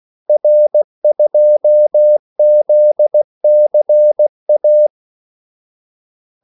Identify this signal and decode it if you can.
Morse code, heard as a keyed tone, about 16 wpm, 600 Hz. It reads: R2ZCA
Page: Kaldesignaler 16 wpm